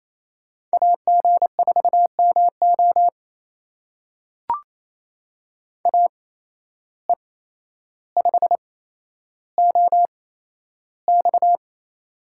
Morse code, heard as a keyed tone, about 27 wpm, spaced as for 12 wpm, 700 Hz. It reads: AG4MO AE5OX
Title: USA Call Signs - 8x Character Spacing - 28wpm